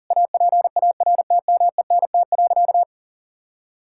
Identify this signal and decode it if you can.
Morse code, heard as a keyed tone, about 37 wpm, 700 Hz. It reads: APARTMENT.
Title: Sentences from Top 1000 Words - Review Part 6 - 40wpm